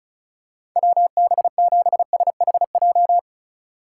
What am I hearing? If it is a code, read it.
Morse code, heard as a keyed tone, about 36 wpm, 700 Hz. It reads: WB7SHJ
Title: USA Call Signs - 6 Characters - 4x Character Spacing - 35wpm